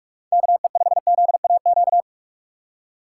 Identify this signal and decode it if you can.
Morse code, heard as a keyed tone, about 46 wpm, 700 Hz. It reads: KE5ZAQ
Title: USA Call Signs - 6 Characters - 4x Character Spacing - 45wpm